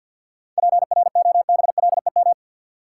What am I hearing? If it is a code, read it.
Morse code, heard as a keyed tone, about 51 wpm, 700 Hz. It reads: PROBLEM
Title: Seven Letter Words - 50wpm